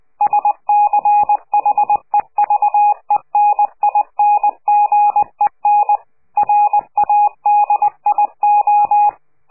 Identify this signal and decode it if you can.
Morse code, heard as a keyed tone, about 20 wpm, 830 Hz. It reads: SCHEVENINGEN RADIO
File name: CQ de PCH laatste bericht farewell